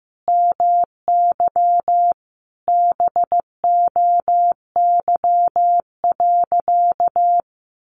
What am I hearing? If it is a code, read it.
Morse code, heard as a keyed tone, about 15 wpm, 700 Hz. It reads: MY BOY.
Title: Sentences from Top 900 Words 15wpm